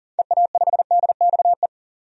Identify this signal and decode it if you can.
Morse code, heard as a keyed tone, about 40 wpm, 700 Hz. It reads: EA5DXE
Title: DXCC-100 Call Signs - 6 Characters - 40wpm